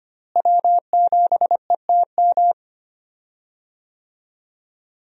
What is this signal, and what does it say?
Morse code, heard as a keyed tone, about 25 wpm, 700 Hz. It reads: W7ETM
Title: USA Call Signs - 5 Characters - Repeated 2x - 25wpm